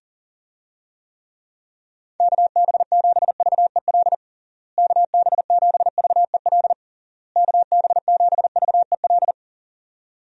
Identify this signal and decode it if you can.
Morse code, heard as a keyed tone, about 40 wpm, 700 Hz. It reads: KB7VEL KB7VEL KB7VEL
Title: USA Call Signs - 6 Characters - Repeated 3x - 40wpm